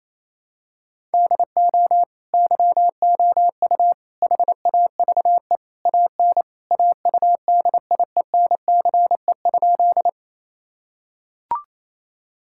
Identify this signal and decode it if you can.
Morse code, heard as a keyed tone, about 28 wpm, 700 Hz. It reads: DO YOU HAVE AN AUDIENCE?
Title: Sentences from Top 1000 Words - Review Part 5 - 28wpm